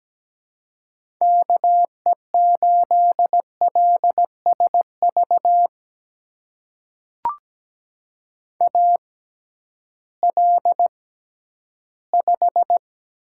Morse code, heard as a keyed tone, about 17 wpm, spaced as for 11 wpm, 700 Hz. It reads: KE8LSV AL5
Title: USA Call Signs - 6x Character Spacing - 17wpm